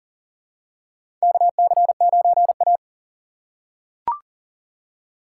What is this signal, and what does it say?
Morse code, heard as a keyed tone, about 40 wpm, 700 Hz. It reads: KC9A
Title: USA Call Signs - 6x Character Spacing - 40wpm